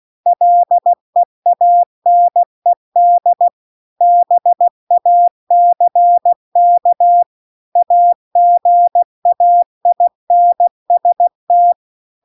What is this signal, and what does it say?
Morse code, heard as a keyed tone, about 16 wpm, 700 Hz. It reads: LEANED BACK AGAINST